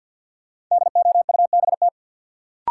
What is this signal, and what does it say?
Morse code, heard as a keyed tone, about 51 wpm, 700 Hz. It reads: DOUBT
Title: Sentences from Top 1500 Words - New Word Drill - 50wpm